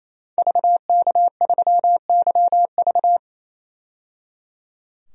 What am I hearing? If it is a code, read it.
Morse code, heard as a keyed tone, about 27 wpm, 700 Hz. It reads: VK3YV
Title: DXCC-100 Call Signs - 5 Characters - 28wpm